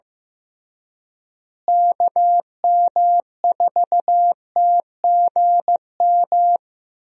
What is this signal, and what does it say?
Morse code, heard as a keyed tone, about 16 wpm, 700 Hz. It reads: KM4TGM
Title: USA Call Signs - 6 Characters - Repeated 2x - 15wpm